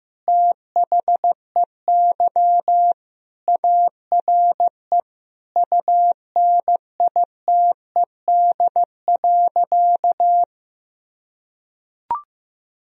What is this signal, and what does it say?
Morse code, heard as a keyed tone, about 15 wpm, 700 Hz. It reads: THEY ARE UNITED.
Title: Sentences from Top 1000 Words - Review Part 4 - 15wpm